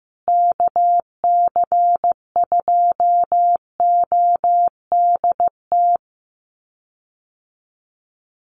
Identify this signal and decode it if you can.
Morse code, heard as a keyed tone, about 15 wpm, 700 Hz. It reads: KC2ODT
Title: USA Call Signs Repeated 2x 15wpm